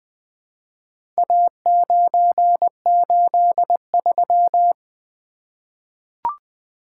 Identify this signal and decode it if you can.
Morse code, heard as a keyed tone, about 20 wpm, 700 Hz. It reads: A983
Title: Four Letter-Number Combinations - 20wpm